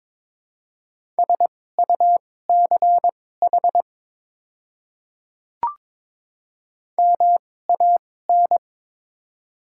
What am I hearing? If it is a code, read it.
Morse code, heard as a keyed tone, about 23 wpm, spaced as for 16 wpm, 700 Hz. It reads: SUCH MAN
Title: 2x Character Spacing - Top 300 Words - 22wpm